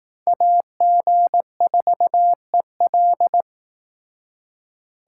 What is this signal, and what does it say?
Morse code, heard as a keyed tone, about 18 wpm, 700 Hz. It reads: AG4EL
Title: Speed Racing - USA 2x2 Call Signs 15wpm